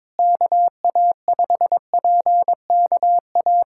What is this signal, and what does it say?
Morse code, heard as a keyed tone, about 22 wpm, 700 Hz. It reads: KA5PKA
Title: USA Callsign DE Callsign - 6 Characters - 22wpm